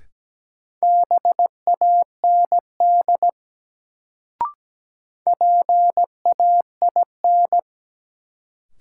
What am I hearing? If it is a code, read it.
Morse code, heard as a keyed tone, about 17 wpm, 700 Hz. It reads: BAND PAIN
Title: Sentences from Top 1600 Words - New Word Drill - 17wpm